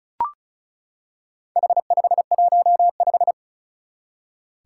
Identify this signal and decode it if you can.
Morse code, heard as a keyed tone, about 36 wpm, 700 Hz. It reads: H515
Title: Four Challenging Letter-Number Combinations 35wpm